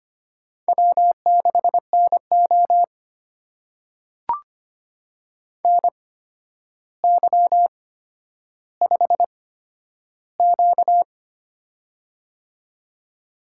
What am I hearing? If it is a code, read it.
Morse code, heard as a keyed tone, about 25 wpm, spaced as for 11 wpm, 700 Hz. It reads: W6NO NY5Q
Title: USA Call Signs - 4 Characters - 8x Character Spacing - 25wpm